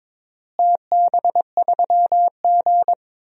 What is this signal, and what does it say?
Morse code, heard as a keyed tone, about 22 wpm, 700 Hz. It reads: TB3G
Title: DXCC-100 Call Signs - 4 Characters - Repeated 2x - 22wpm